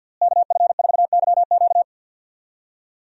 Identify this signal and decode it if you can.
Morse code, heard as a keyed tone, about 47 wpm, 700 Hz. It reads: KF4YQ
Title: USA Call Signs - 5 Characters - Repeated 2x - 50wpm